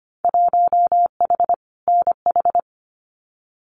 Morse code, heard as a keyed tone, about 25 wpm, 700 Hz. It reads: 1H NH
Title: Winter Field Day Exchanges 25wpm